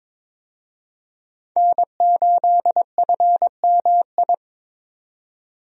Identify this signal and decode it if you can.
Morse code, heard as a keyed tone, about 22 wpm, 700 Hz. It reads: N8FMI
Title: USA Call Signs - 5 Characters - 22wpm